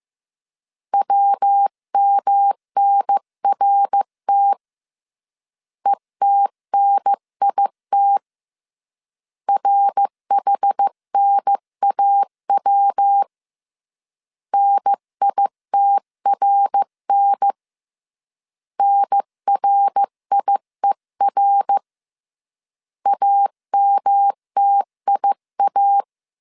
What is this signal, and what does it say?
Morse code, heard as a keyed tone, about 15 wpm, 800 Hz. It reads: WMNRT ETNIT RHNAW NITRN NRIER AMTIA
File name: CW / DET / 1 3 1500C S A 14wpm 2447